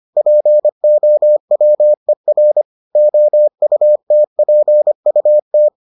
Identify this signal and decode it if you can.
Morse code, heard as a keyed tone, about 25 wpm, 600 Hz. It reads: POWER OUTPUT